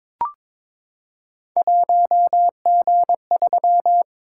Morse code, heard as a keyed tone, about 21 wpm, 700 Hz. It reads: 1G3
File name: practice / 22wpm / Three Letter-Number Combinations - 22wpm